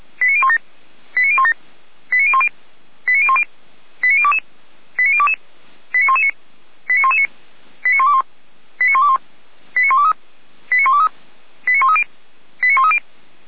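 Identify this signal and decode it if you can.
Five-tone selective call (ZVEI), heard as quick runs of five tones, tones of about 70 ms. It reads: 89017 89017 89010 89010 89022 89022 89119 89119 89121 89121 89123 89123 89130 89130
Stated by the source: ZVEI five-tone selective call 89014, received by SDR.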